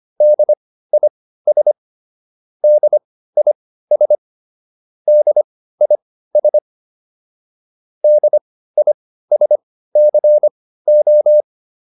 Morse code, heard as a keyed tone, about 25 wpm, spaced as for 15 wpm, 600 Hz. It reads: DIS DIS DIS DISCO